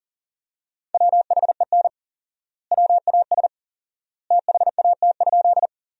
Morse code, heard as a keyed tone, about 37 wpm, 700 Hz. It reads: WHEN WAS THAT?
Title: 4x Word-Spacing - Sentences from Top 100 Words - 40wpm